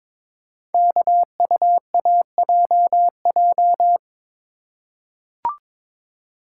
Morse code, heard as a keyed tone, about 22 wpm, 700 Hz. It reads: KUAJJ
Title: Code Groups - 5 Letters - 22wpm